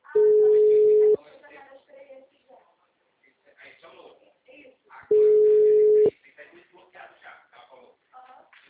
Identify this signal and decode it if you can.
Ringback tone (Europe, 425 Hz), with speech.